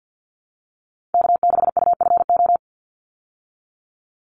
Morse code, heard as a keyed tone, about 50 wpm, 700 Hz. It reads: K6UFO
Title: Call Signs Repeated 3x - 50wpm